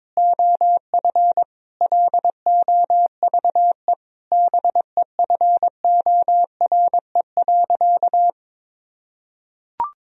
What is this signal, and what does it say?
Morse code, heard as a keyed tone, about 22 wpm, 700 Hz. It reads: OF LOVE BEFORE.
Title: Sentences from Top 1000 Words - Review Part 3 - 22wpm